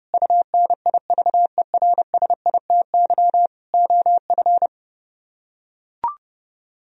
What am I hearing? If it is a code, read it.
Morse code, heard as a keyed tone, about 30 wpm, 700 Hz. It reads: UNIVERSITY OF